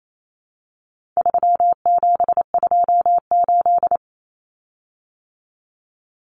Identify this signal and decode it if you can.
Morse code, heard as a keyed tone, about 28 wpm, 700 Hz. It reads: 3728
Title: Common Cut Numbers - 4-Digit 28wpm